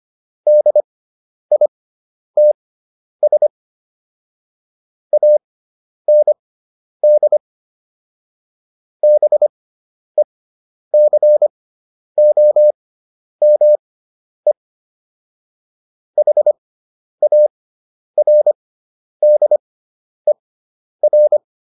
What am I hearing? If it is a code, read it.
Morse code, heard as a keyed tone, about 25 wpm, spaced as for 10 wpm, 600 Hz. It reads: DITS AND BECOME HARDER